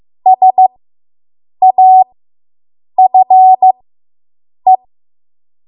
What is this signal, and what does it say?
Morse code, heard as a keyed tone, about 15 wpm, spaced as for 7 wpm, 750 Hz. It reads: SAFE